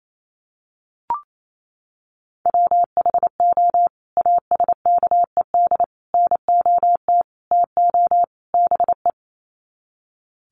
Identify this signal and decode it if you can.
Morse code, heard as a keyed tone, about 28 wpm, 700 Hz. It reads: WHO ASKED NOT TO BE
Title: Sets of 5 Words Encore2 28wpm